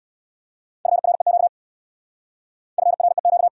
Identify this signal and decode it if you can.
Morse code, heard as a keyed tone, about 69 wpm, 700 Hz. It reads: FREQ FREQ